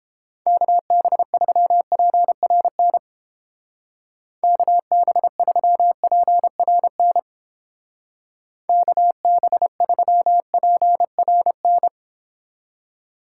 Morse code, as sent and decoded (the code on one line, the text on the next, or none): -.- -... ...-- .--. .-. -. / -.- -... ...-- .--. .-. -. / -.- -... ...-- .--. .-. -.
KB3PRN KB3PRN KB3PRN